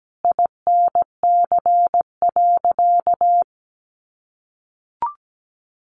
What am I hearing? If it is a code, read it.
Morse code, heard as a keyed tone, about 17 wpm, 700 Hz. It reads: INC.
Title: A Sound Of Thunder 17wpm